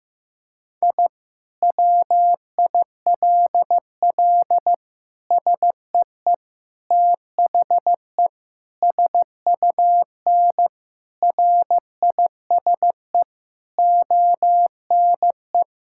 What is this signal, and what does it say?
Morse code, heard as a keyed tone, about 15 wpm, 700 Hz. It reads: I WILL SEE THE SUN RISE ONE
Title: Sentences from Top 1000 Words - Review Part 6 - 15wpm